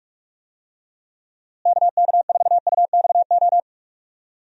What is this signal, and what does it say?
Morse code, heard as a keyed tone, about 41 wpm, 700 Hz. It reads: KK4UXO
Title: USA Call Signs - 6 Characters - Repeated 3x - 45wpm